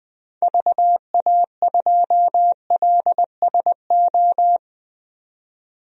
Morse code, heard as a keyed tone, about 20 wpm, 700 Hz. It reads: VA2LSO
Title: DXCC-100 Call Signs - 6 Characters - 20wpm